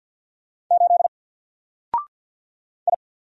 Morse code, heard as a keyed tone, about 47 wpm, 700 Hz. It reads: 8 I
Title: Single Letter-Number Alt 50wpm